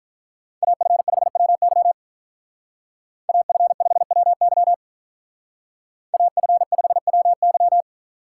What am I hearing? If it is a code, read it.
Morse code, heard as a keyed tone, about 46 wpm, 700 Hz. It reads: AF5WY AF5WY AF5WY